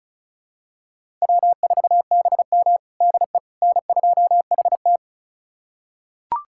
Morse code, heard as a keyed tone, about 36 wpm, 700 Hz. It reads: W4BM DE N2HT
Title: USA Callsign DE Callsign - 4 Characters - 35wpm